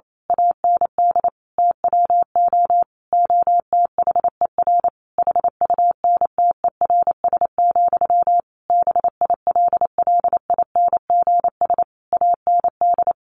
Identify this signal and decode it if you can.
Morse code, heard as a keyed tone, about 28 wpm, 700 Hz. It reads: AND TWO OTHER HUNTERS, BILLINGS AND
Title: A Sound Of Thunder 28wpm